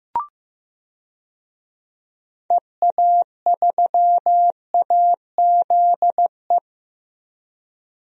Morse code, heard as a keyed tone, about 15 wpm, 700 Hz. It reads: EA3AZE
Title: DXCC-100 Call Signs - 6 Characters - Repeated 3x - 15wpm